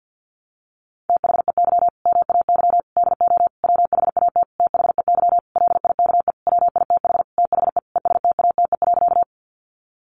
Morse code, heard as a keyed tone, about 48 wpm, 700 Hz. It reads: THEY MAY DO WHAT THEY LIKE WITH THE ESTATE.
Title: Sentences from Top 1200 Words 50wpm